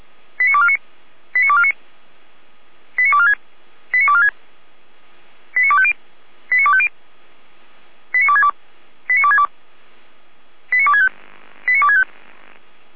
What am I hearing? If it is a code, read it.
Five-tone selective call (ZVEI), heard as quick runs of five tones, tones of about 70 ms. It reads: 89249 89249 89246 89246 89250 89250 89262 89262 89265 89265